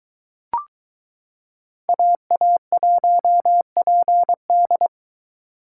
Morse code, heard as a keyed tone, about 22 wpm, 700 Hz. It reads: AA1PD